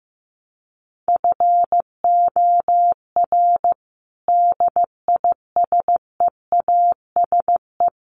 Morse code, heard as a keyed tone, about 15 wpm, 700 Hz. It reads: FOR DISEASE
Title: Sets of 5 Words Encore2 15wpm